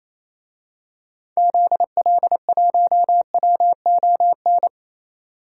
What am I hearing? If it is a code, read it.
Morse code, heard as a keyed tone, about 27 wpm, 700 Hz. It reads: ZL1WON
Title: DXCC-100 Call Signs - 6 Characters - 28wpm